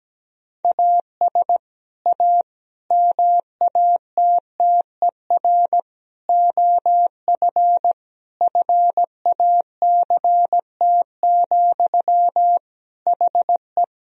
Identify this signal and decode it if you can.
Morse code, heard as a keyed tone, about 17 wpm, 700 Hz. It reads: AS A MATTER OF FACT, HE
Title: Sentences from Top 1000 Words - Review Part 3 - 17wpm